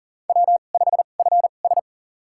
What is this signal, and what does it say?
Morse code, heard as a keyed tone, about 43 wpm, spaced as for 30 wpm, 700 Hz. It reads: W5FS
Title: USA Call Signs - 4 Characters - 2x Character Spacing - 40wpm